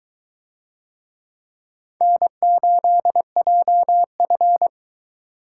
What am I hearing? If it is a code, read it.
Morse code, heard as a keyed tone, about 22 wpm, 700 Hz. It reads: N8JF